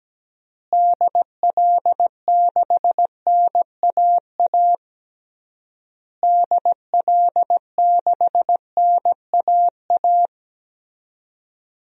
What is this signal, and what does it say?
Morse code, heard as a keyed tone, about 17 wpm, 700 Hz. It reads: DL6NAA DL6NAA
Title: DXCC-100 Call Signs - 6 Characters - Repeated 3x - 17wpm